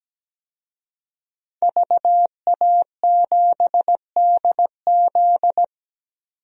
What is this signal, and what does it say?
Morse code, heard as a keyed tone, about 17 wpm, 700 Hz. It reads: VA7DZ
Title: DXCC-100 Call Signs - 5 Characters - 17wpm